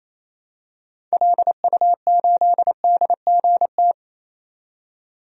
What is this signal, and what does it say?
Morse code, heard as a keyed tone, about 28 wpm, 700 Hz. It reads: LU8DGT
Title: DXCC-100 Call Signs - 6 Characters - 28wpm